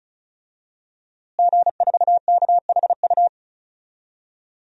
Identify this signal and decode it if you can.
Morse code, heard as a keyed tone, about 36 wpm, 700 Hz. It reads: G4KHU
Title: DXCC-100 Call Signs - 5 Characters - Repeated 3x - 35wpm